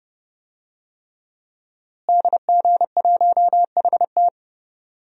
Morse code, heard as a keyed tone, about 30 wpm, 700 Hz. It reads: DG1HT
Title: DXCC-100 Call Signs - 5 Characters - Repeated 3x - 30wpm